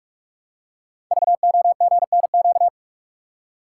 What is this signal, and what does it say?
Morse code, heard as a keyed tone, about 44 wpm, 700 Hz. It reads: VOGNQ
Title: Code Groups - 5 Letters - 45wpm